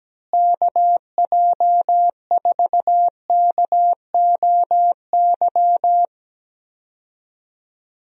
Morse code, heard as a keyed tone, about 17 wpm, 700 Hz. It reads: KJ4KOY
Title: USA Call Signs - 6 Characters - Repeated 3x - 17wpm